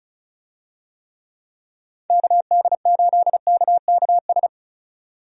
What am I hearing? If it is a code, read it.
Morse code, heard as a keyed tone, about 34 wpm, 700 Hz. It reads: KD8KKS
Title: USA Call Signs - 6 Characters - 35wpm